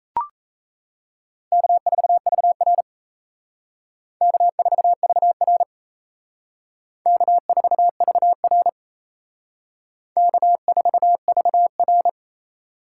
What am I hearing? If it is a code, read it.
Morse code, heard as a keyed tone, about 35 wpm, 700 Hz. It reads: K4VR K4VR K4VR K4VR
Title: Speed Racing - USA Call Signs 28wpm